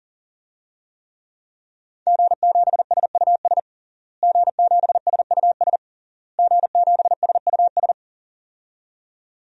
Morse code, heard as a keyed tone, about 40 wpm, 700 Hz. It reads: G7SUS G7SUS G7SUS
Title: DXCC-100 Call Signs - 5 Characters - Repeated 3x - 40wpm